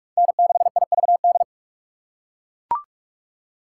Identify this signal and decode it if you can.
Morse code, heard as a keyed tone, about 44 wpm, 700 Hz. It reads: N6IVD